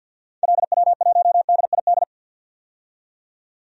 Morse code, heard as a keyed tone, about 52 wpm, 700 Hz. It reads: LW1DID
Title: DXCC-100 Call Signs - 6 Characters - 50wpm